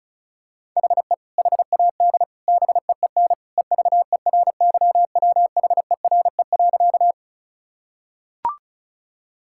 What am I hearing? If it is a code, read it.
Morse code, heard as a keyed tone, about 36 wpm, 700 Hz. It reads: HE HAD BEEN EVERYWHERE.
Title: Sentences from Top 1600 Words 35wpm